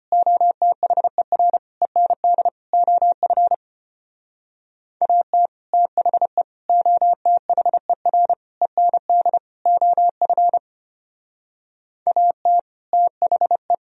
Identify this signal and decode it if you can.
Morse code, heard as a keyed tone, about 30 wpm, 700 Hz. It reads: OTHER END OF AT THE OTHER END OF AT THE
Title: Speed-Racing - Sets of 5 Words 25wpm